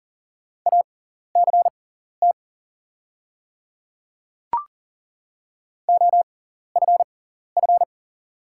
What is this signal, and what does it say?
Morse code, heard as a keyed tone, about 40 wpm, spaced as for 14 wpm, 690 Hz. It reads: ACT OFF
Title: 6x Character Spacing - Top 300 Words - 40wpm